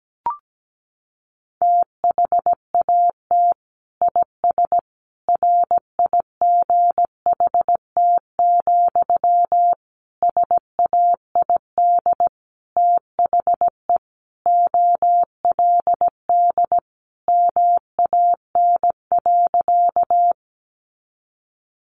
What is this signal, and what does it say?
Morse code, heard as a keyed tone, about 17 wpm, 700 Hz. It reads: THAT IS RIGHT, SAID THE OLD MAN.
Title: Sentences from Top 200 Words 17wpm